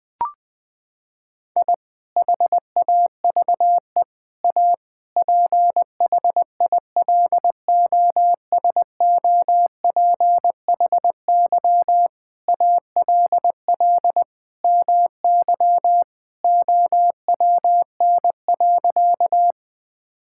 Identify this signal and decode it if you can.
Morse code, heard as a keyed tone, about 21 wpm, 700 Hz. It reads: I HAVE A PHILOSOPHY ALL MY OWN.
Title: Sentences from Top 1000 Words - Review Part 6 - 20wpm